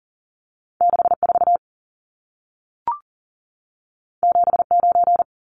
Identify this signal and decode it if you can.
Morse code, heard as a keyed tone, about 40 wpm, 700 Hz. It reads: 64 79
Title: Numbers - 2-Digit 40wpm